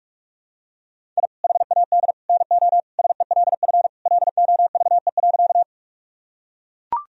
Morse code, heard as a keyed tone, about 41 wpm, 700 Hz. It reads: I HAD NO SELF LOVE.
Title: Sentences from Top 1000 Words - Review Part 1 - 45wpm